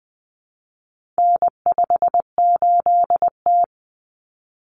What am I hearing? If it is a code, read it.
Morse code, heard as a keyed tone, about 20 wpm, 700 Hz. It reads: N58T
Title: Common Cut Numbers - 4-Digit 20wpm